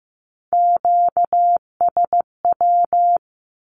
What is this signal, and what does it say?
Morse code, heard as a keyed tone, about 15 wpm, 700 Hz. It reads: QSW
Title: Q-Codes with Spoken Meaning - 15wpm